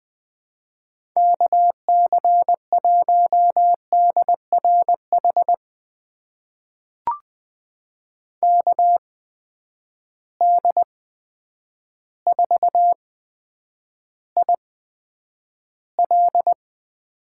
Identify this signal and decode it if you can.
Morse code, heard as a keyed tone, about 21 wpm, spaced as for 9 wpm, 700 Hz. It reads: KC1DRH KD4IL
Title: USA Call Signs - 8x Character Spacing - 20wpm